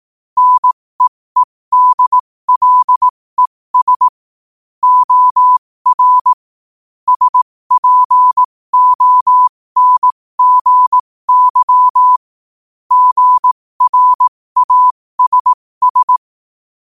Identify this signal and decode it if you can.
Morse code, heard as a keyed tone, about 18 wpm, spaced as for 15 wpm, 1000 Hz. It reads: NEEDLES OR SPONGY GRASS
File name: qotdfiles / 15wpm